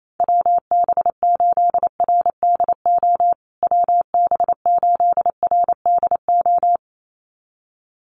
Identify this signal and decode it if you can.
Morse code, heard as a keyed tone, about 28 wpm, 700 Hz. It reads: WB8RDO WB8RDO
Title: USA Call Signs Repeated 2x 28wpm